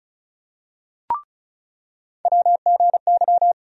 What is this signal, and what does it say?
Morse code, heard as a keyed tone, about 36 wpm, 700 Hz. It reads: WGY
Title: USA Call Sign Postfixes - 3 Letters 35wpm